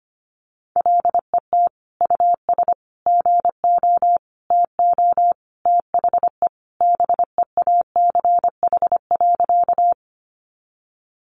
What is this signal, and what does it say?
Morse code, heard as a keyed tone, about 25 wpm, 700 Hz. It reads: LET US GO TO THE BEACH.
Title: Sentences from Top 1500 Words 25wpm